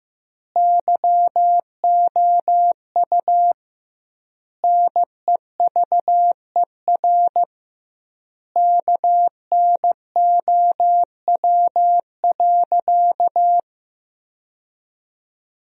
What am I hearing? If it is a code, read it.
Morse code, heard as a keyed tone, about 16 wpm, 700 Hz. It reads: YOU NEVER KNOW.
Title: 2x Word-Spacing - Sentences from Top 300 Words - 15wpm